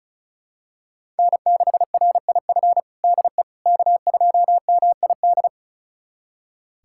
Morse code, heard as a keyed tone, about 36 wpm, 700 Hz. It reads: N6RIF DE K2MID
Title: USA Callsign DE Callsign - 5 Characters - 35wpm